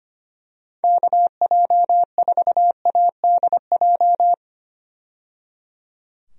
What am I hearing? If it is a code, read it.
Morse code, heard as a keyed tone, about 25 wpm, 700 Hz. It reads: KJ4ADJ
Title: USA Call Signs - 6 Characters - 25wpm